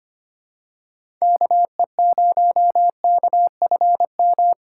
Morse code, heard as a keyed tone, about 25 wpm, 700 Hz. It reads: KE0KFM
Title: USA Call Signs - 6 Characters - 4x Character Spacing - 25wpm